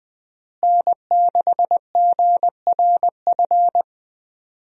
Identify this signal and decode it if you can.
Morse code, heard as a keyed tone, about 21 wpm, 700 Hz. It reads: N6GRF
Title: USA Call Signs - 5 Characters - Repeated 3x - 20wpm